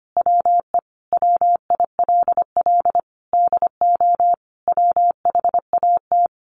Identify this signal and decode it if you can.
Morse code, heard as a keyed tone, about 25 wpm, 700 Hz. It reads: WE WILL DO WHAT
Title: Sentences from Top 900 Words 25wpm